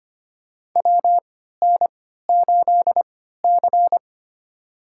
Morse code, heard as a keyed tone, about 25 wpm, spaced as for 14 wpm, 700 Hz. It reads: WN8C
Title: USA Call Signs - 4 Characters - 3x Character Spacing - 25wpm